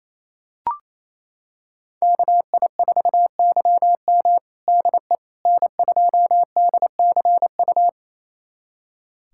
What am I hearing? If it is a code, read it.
Morse code, heard as a keyed tone, about 28 wpm, 700 Hz. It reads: KI4YM DE N2DCU
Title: USA Callsign DE Callsign - 5 Characters - 28wpm